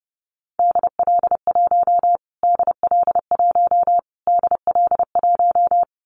Morse code, heard as a keyed tone, about 30 wpm, 700 Hz. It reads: DL1 DL1 DL1
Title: Callsign Prefixes for Top 25 DXCC - Repeated 3x - 30wpm